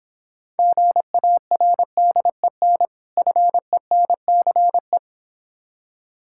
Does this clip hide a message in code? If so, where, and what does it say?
Yes: Morse code from 0.6 to 5.0 s, heard as a keyed tone, about 26 wpm, 700 Hz. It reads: GARDEN FENCE